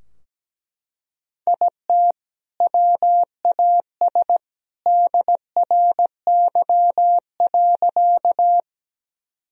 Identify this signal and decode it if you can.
Morse code, heard as a keyed tone, about 17 wpm, 700 Hz. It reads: IT WAS DRY.
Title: Sentences from Top 1000 Words - Review Part 3 - 17wpm